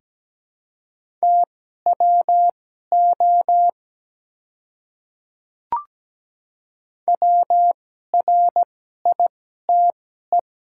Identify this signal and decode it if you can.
Morse code, heard as a keyed tone, about 17 wpm, spaced as for 12 wpm, 690 Hz. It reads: TWO WRITE
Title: 2x Character Spacing - Top 100 Words - 17wpm